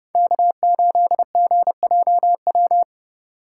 Morse code, heard as a keyed tone, about 29 wpm, 700 Hz. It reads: K8GJW